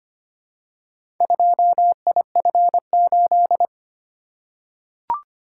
Morse code, heard as a keyed tone, about 25 wpm, 700 Hz. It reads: 2IF8
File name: practice / 25wpm / Four Letter-Number Combinations - 25wpm